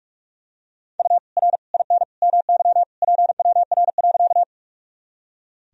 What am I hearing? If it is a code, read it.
Morse code, heard as a keyed tone, about 46 wpm, 700 Hz. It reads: U R IN MY PWR.
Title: Sentences from Top 500 Words and Abbreviations 45wpm